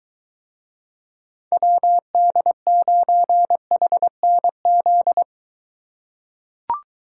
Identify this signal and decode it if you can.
Morse code, heard as a keyed tone, about 23 wpm, 700 Hz. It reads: WD9HNZ